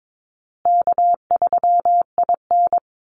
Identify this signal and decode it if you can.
Morse code, heard as a keyed tone, about 22 wpm, 700 Hz. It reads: K3IN
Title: Easy Call Signs Repeated 2x - 22wpm